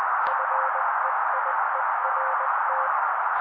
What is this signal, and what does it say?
Morse code, heard as a keyed tone, about 20 wpm, 530 Hz. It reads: FEIERT
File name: daily / 18wpm noise